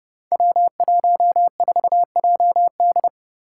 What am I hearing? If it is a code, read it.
Morse code, heard as a keyed tone, about 30 wpm, 700 Hz. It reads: W14JD
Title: Five Challenging Letter-Number Combinations 30wpm